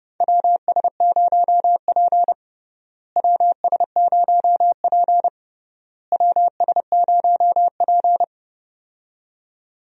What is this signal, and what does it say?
Morse code, heard as a keyed tone, about 32 wpm, 700 Hz. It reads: WS0P WS0P WS0P
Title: USA Call Signs - 4 Characters - Repeated 3x - 30wpm